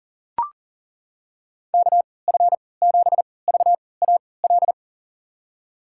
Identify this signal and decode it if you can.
Morse code, heard as a keyed tone, about 37 wpm, spaced as for 22 wpm, 700 Hz. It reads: KF7VAL
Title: USA Call Signs - 6 Characters - 3x Character Spacing - 40wpm